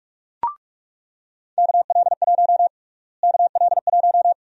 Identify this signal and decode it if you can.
Morse code, heard as a keyed tone, about 42 wpm, 700 Hz. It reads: KL1 KL1
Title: USA Call Sign Prefixes - Repeated 2x - 45wpm